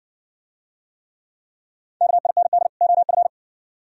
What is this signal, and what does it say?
Morse code, heard as a keyed tone, about 57 wpm, 700 Hz. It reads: KIND OF